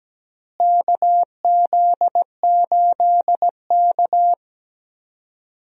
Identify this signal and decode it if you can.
Morse code, heard as a keyed tone, about 17 wpm, 700 Hz. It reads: KZ8K
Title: USA Call Signs - 4 Characters - Repeated 3x - 17wpm